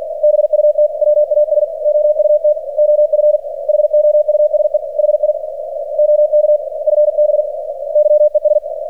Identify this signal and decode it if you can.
Morse code, heard as a keyed tone, about 46 wpm, 600 Hz, with radio noise. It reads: BUT WAS OUT OF HOUSE HI OK FB YES
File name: archived / CRSnet